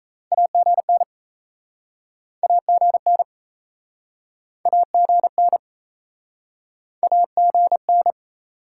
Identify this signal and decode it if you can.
Morse code, heard as a keyed tone, about 38 wpm, 700 Hz. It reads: AGN AGN AGN AGN